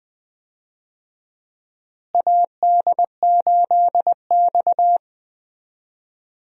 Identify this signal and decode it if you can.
Morse code, heard as a keyed tone, about 20 wpm, 700 Hz. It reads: AD8X
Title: USA Call Signs - 4 Characters - Repeated 3x - 20wpm